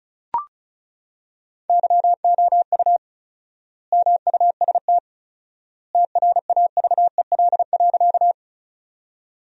Morse code, heard as a keyed tone, about 36 wpm, 700 Hz. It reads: YOU MUST TRAVEL.
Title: 4x Word-Spacing - Sentences from Top 400 Words - 35wpm